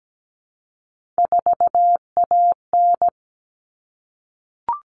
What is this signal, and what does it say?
Morse code, heard as a keyed tone, about 17 wpm, 700 Hz. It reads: 4AN
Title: All Cut Numbers - 3-Digit 17wpm